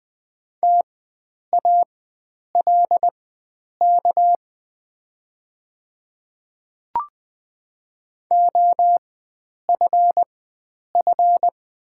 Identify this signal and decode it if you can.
Morse code, heard as a keyed tone, about 20 wpm, spaced as for 9 wpm, 700 Hz. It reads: TALK OFF
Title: 4x Character Spacing - Top 500 Words - 20wpm